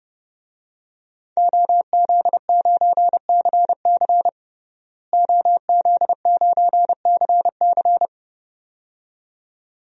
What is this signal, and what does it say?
Morse code, heard as a keyed tone, about 30 wpm, 700 Hz. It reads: OZ9CC OZ9CC
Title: DXCC-100 Call Signs - 5 Characters - Repeated 2x - 30wpm